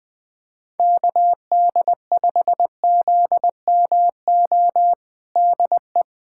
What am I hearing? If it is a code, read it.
Morse code, heard as a keyed tone, about 20 wpm, 700 Hz. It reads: KD5ZMO DE
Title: USA Callsign DE Callsign - 6 Characters - 20wpm